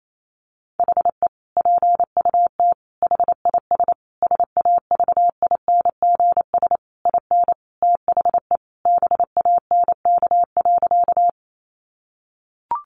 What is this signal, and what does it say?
Morse code, heard as a keyed tone, about 28 wpm, 700 Hz. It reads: HE PUT HIS SAVINGS IN THE BANK.
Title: Sentences from Top 1200 Words 28wpm